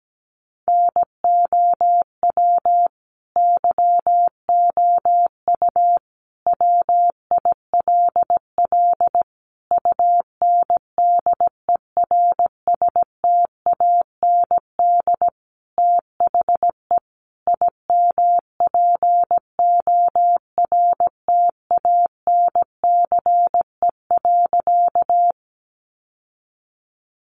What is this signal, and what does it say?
Morse code, heard as a keyed tone, about 17 wpm, 700 Hz. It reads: NOW YOU WILL UNDERSTAND THE IMPORTANCE.
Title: Sentences from Top 1200 Words 17wpm